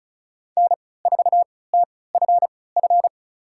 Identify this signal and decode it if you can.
Morse code, heard as a keyed tone, about 36 wpm, spaced as for 20 wpm, 690 Hz. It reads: N4TFF
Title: USA Call Signs - 5 Characters - 3x Character Spacing - 35wpm